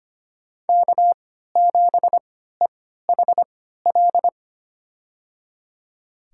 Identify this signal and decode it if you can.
Morse code, heard as a keyed tone, about 25 wpm, spaced as for 14 wpm, 700 Hz. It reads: K7EHL
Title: USA Call Signs - 5 Characters - 3x Character Spacing - 25wpm